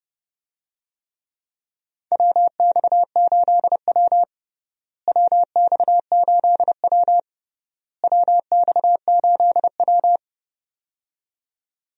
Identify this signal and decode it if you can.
Morse code, heard as a keyed tone, about 30 wpm, 700 Hz. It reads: WX8W WX8W WX8W
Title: USA Call Signs - 4 Characters - Repeated 3x - 30wpm